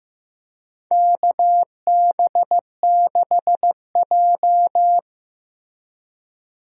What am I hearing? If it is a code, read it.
Morse code, heard as a keyed tone, about 16 wpm, 700 Hz. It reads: KB6J
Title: USA Call Signs - 4 Characters - Repeated 2x - 15wpm